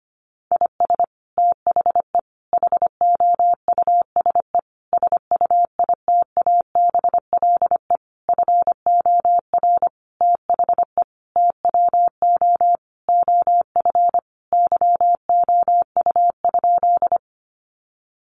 Morse code, heard as a keyed tone, about 25 wpm, 700 Hz. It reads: IS THE HOUSE SUITABLE FOR THE TWO OF YOU?
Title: Sentences from Top 1400 Words 25wpm